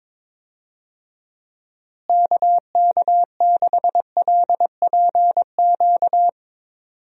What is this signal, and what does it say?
Morse code, heard as a keyed tone, about 22 wpm, 700 Hz. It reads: KK6LPQ
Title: USA Call Signs - 6 Characters - Repeated 2x - 22wpm